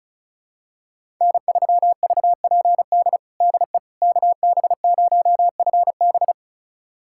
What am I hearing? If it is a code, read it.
Morse code, heard as a keyed tone, about 36 wpm, 700 Hz. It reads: N3VPD DE KB0FB
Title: USA Callsign DE Callsign - 5 Characters - 35wpm